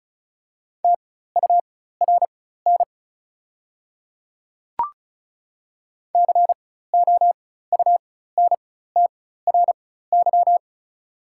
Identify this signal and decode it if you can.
Morse code, heard as a keyed tone, about 36 wpm, spaced as for 16 wpm, 690 Hz. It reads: TURN COUNTRY
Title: 4x Character Spacing - Top 300 Words - 35wpm